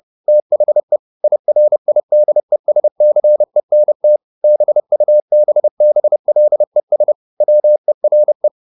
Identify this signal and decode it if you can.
Morse code, heard as a keyed tone, about 30 wpm, 600 Hz. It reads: THE IRIDESCENT BUBBLES WERE
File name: the old man and the sea / the old man and the sea 1 30wpm